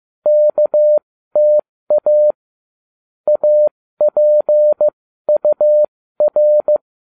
Morse code, heard as a keyed tone, about 15 wpm, spaced as for 13 wpm, 600 Hz. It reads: KTA APUR